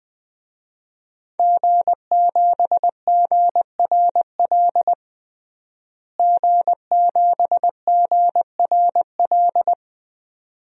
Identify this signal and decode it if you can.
Morse code, heard as a keyed tone, about 21 wpm, 700 Hz. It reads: G7GRL G7GRL
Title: DXCC-100 Call Signs - 5 Characters - Repeated 2x - 20wpm